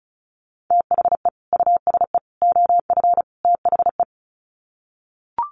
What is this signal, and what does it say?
Morse code, heard as a keyed tone, about 34 wpm, 700 Hz. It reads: THE USE OF THE
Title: Sets of 4 Words 35wpm